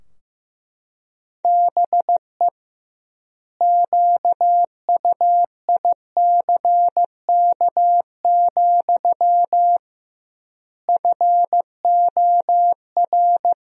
Morse code, heard as a keyed tone, about 15 wpm, 700 Hz. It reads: BE QUICK, FOR
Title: 2x Word-Spacing - Sentences from Top 500 Words - 15wpm